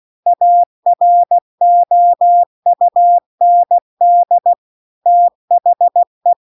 Morse code, heard as a keyed tone, about 16 wpm, 700 Hz. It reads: AROUND THE